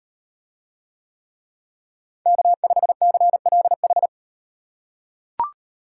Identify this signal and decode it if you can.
Morse code, heard as a keyed tone, about 38 wpm, 700 Hz. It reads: K5CLH